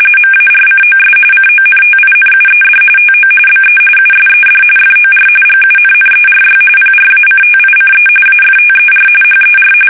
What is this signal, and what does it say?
RTTY, about 45.45 baud. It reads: MARCOS CONTINUED TO BE USED FOR GUNNERY PRACTICE AFTER HER
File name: fsk850 / battleships